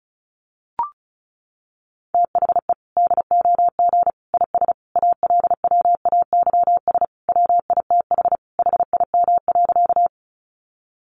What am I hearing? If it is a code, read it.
Morse code, heard as a keyed tone, about 34 wpm, 700 Hz. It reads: THE DOG IS ALWAYS WITH HIM.
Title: Sentences from Top 400 Words 35wpm